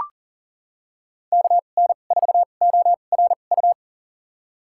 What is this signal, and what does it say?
Morse code, heard as a keyed tone, about 40 wpm, spaced as for 29 wpm, 700 Hz. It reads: KN4ORU
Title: USA Call Signs - 6 Characters - 2x Character Spacing - 40wpm